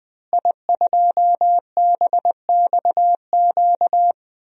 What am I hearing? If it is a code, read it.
Morse code, heard as a keyed tone, about 20 wpm, 700 Hz. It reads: I2BXQ
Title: DXCC-100 Call Signs - 5 Characters - Repeated 3x - 20wpm